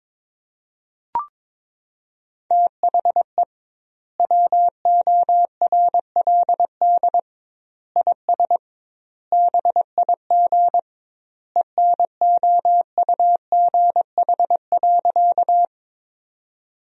Morse code, heard as a keyed tone, about 22 wpm, 700 Hz. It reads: THE WORLD IS BIG ENOUGH.
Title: 2x Word-Spacing - Sentences from Top 400 Words - 22wpm